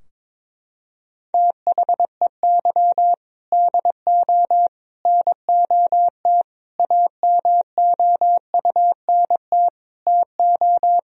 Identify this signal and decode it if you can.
Morse code, heard as a keyed tone, about 23 wpm, 700 Hz. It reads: THEY DO NOT AMOUNT TO
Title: Sentences from Top 1000 Words - Review Part 4 - 22wpm